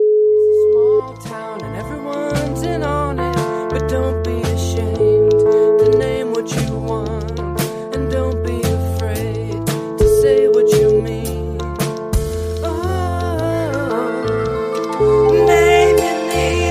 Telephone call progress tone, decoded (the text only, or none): ringback tone